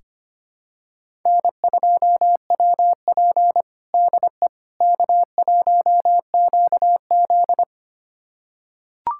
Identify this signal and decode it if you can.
Morse code, heard as a keyed tone, about 25 wpm, 700 Hz. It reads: N2WP DE K1QZ
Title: USA Callsign DE Callsign - 4 Characters - 25wpm